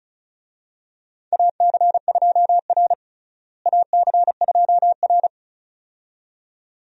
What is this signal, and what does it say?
Morse code, heard as a keyed tone, about 34 wpm, 700 Hz. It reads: AC2R AC2R
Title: USA Call Signs - 4 Characters - Repeated 2x - 35wpm